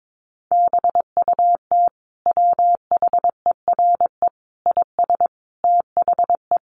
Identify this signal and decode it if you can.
Morse code, heard as a keyed tone, about 22 wpm, 700 Hz. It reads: BUT WHERE IS THE
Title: Sentences from Top 1500 Words 22wpm